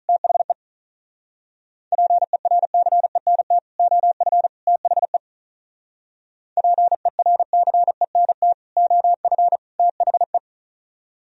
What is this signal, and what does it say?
Morse code, heard as a keyed tone, about 44 wpm, 700 Hz. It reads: THE PERCENT OF THE PERCENT OF THE